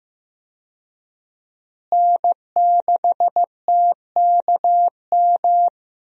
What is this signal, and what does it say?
Morse code, heard as a keyed tone, about 15 wpm, 700 Hz. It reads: N6TKM
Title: USA Call Signs - 5 Characters - Repeated 2x - 15wpm